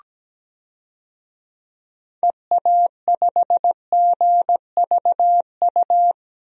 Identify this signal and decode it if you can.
Morse code, heard as a keyed tone, about 17 wpm, 700 Hz. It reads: EA5GVU
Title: DXCC-100 Call Signs - 6 Characters - 17wpm